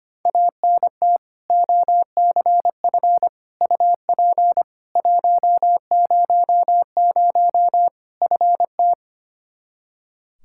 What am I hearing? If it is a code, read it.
Morse code, heard as a keyed tone, about 25 wpm, 700 Hz. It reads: ANT OCF UP 100 FT
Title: Practice QSO - Exchange 2 - Info - 25wpm